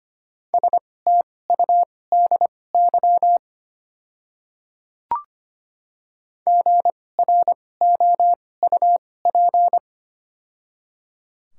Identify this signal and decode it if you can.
Morse code, heard as a keyed tone, about 25 wpm, spaced as for 18 wpm, 700 Hz. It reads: STUDY GROUP
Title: 2x Character Spacing - Top 500 Words - 25wpm